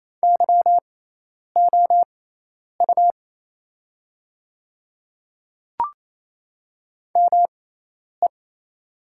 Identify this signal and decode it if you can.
Morse code, heard as a keyed tone, about 28 wpm, spaced as for 10 wpm, 690 Hz. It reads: YOU ME
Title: 6x Character Spacing - Top 200 Words - 28wpm